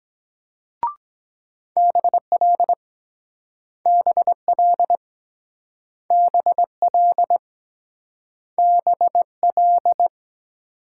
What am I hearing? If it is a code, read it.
Morse code, heard as a keyed tone, about 23 wpm, 700 Hz. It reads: BL BL BL BL